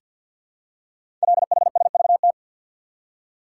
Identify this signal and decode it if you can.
Morse code, heard as a keyed tone, about 48 wpm, 700 Hz. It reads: LHSVT
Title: Code Groups - 5 Letters - 50wpm